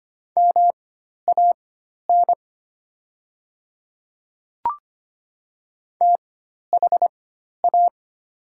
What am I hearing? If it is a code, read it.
Morse code, heard as a keyed tone, about 25 wpm, spaced as for 12 wpm, 700 Hz. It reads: MAN THA
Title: 4x Character Spacing - Top 200 Words - 25wpm